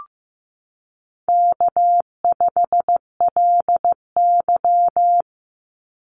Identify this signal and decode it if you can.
Morse code, heard as a keyed tone, about 15 wpm, 700 Hz. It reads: K5LY
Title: Call Signs Repeated 2x - 15wpm